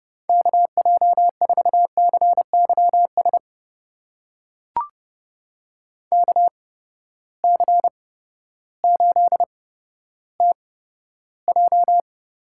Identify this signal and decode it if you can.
Morse code, heard as a keyed tone, about 30 wpm, spaced as for 14 wpm, 700 Hz. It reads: KJ4CYS KC8TJ
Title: USA Call Signs - 6 Characters - 8x Character Spacing - 30wpm